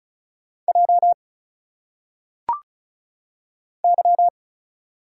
Morse code, heard as a keyed tone, about 36 wpm, 700 Hz. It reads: J Y
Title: Single Letters - Phonetic Pronunciation 35wpm